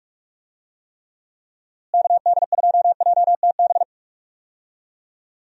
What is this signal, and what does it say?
Morse code, heard as a keyed tone, about 46 wpm, 700 Hz. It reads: KD2JTB
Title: USA Call Signs - 6 Characters - Repeated 3x - 45wpm